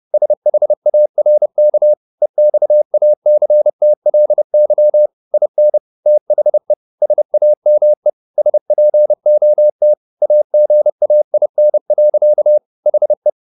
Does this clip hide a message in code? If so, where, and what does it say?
Yes: Morse code from 0.1 to 13.3 s, heard as a keyed tone, about 30 wpm, 600 Hz. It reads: SHARK EXACTLY IN THE SAME SPOT AGAIN. HE